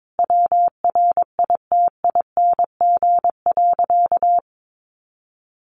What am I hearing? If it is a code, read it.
Morse code, heard as a keyed tone, about 22 wpm, 700 Hz. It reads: WRITING.
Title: Sentences from Top 800 Words 22wpm